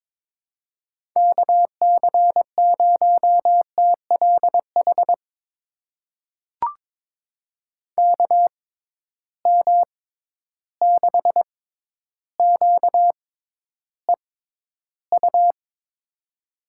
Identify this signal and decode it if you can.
Morse code, heard as a keyed tone, about 22 wpm, spaced as for 11 wpm, 700 Hz. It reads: KC0TLH KM6QEU
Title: USA Call Signs - 6 Characters - 6x Character Spacing - 22wpm